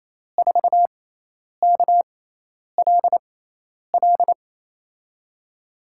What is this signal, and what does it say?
Morse code, heard as a keyed tone, about 27 wpm, spaced as for 10 wpm, 700 Hz. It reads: 4KLL
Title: USA Call Signs - 6x Character Spacing - 28wpm